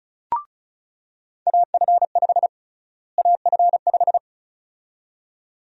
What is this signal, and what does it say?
Morse code, heard as a keyed tone, about 36 wpm, 700 Hz. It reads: AF5 AF5
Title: USA Call Sign Prefixes - Repeated 2x - 35wpm